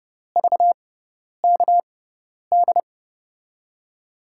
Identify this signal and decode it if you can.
Morse code, heard as a keyed tone, about 32 wpm, spaced as for 10 wpm, 700 Hz. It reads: VKD
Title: USA Call Signs - 6 Characters - 6x Character Spacing - 30wpm